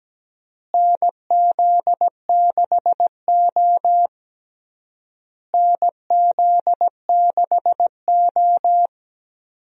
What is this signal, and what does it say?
Morse code, heard as a keyed tone, about 17 wpm, 700 Hz. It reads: NZ6O NZ6O
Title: USA Call Signs - 4 Characters - Repeated 3x - 17wpm